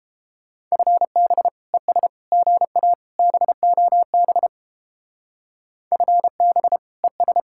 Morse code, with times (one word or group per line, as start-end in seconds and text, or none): 0.7-1.5 FB
1.7-2.1 ES
2.3-2.9 GA
3.2-4.5 BOB
5.9-6.8 FB
7.0-7.4 ES